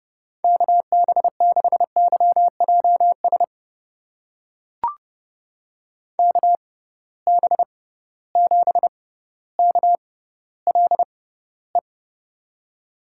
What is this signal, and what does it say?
Morse code, heard as a keyed tone, about 30 wpm, spaced as for 15 wpm, 700 Hz. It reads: KB6YJS KB7KLE
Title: USA Call Signs - 6 Characters - 6x Character Spacing - 30wpm